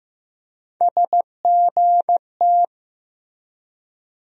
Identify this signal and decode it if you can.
Morse code, heard as a keyed tone, about 15 wpm, 700 Hz. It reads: SGT
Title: USA Call Sign Postfixes - 3 Letters 15wpm